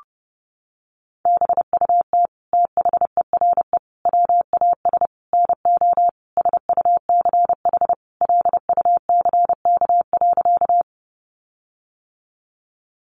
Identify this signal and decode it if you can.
Morse code, heard as a keyed tone, about 30 wpm, 700 Hz. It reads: BUT THERE WAS NO SUCH LUCK.
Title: Sentences from Top 1500 Words 30wpm